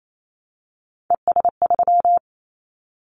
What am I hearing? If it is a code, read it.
Morse code, heard as a keyed tone, about 28 wpm, 700 Hz. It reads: ES3
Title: Callsign Prefixes for Top 100 DXCC - 28wpm